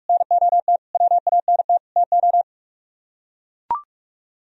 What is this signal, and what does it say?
Morse code, heard as a keyed tone, about 44 wpm, 700 Hz. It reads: NOT WANT TO